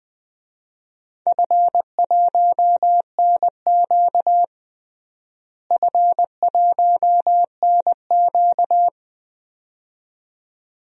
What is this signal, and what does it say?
Morse code, heard as a keyed tone, about 21 wpm, 700 Hz. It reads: F1NQ F1NQ
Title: DXCC-100 Call Signs - 4 Characters - Repeated 2x - 20wpm